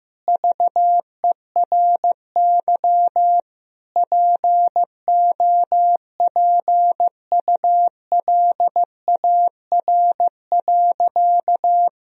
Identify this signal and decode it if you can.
Morse code, heard as a keyed tone, about 15 wpm, 700 Hz. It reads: VERY POPULAR.
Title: Sentences from Top 1000 Words - Review Part 3 - 15wpm